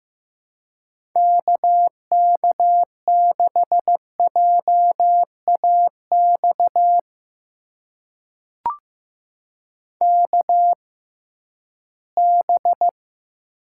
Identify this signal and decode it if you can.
Morse code, heard as a keyed tone, about 15 wpm, spaced as for 11 wpm, 700 Hz. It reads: KK6JAX KB
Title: USA Call Signs - 6 Characters - 6x Character Spacing - 15wpm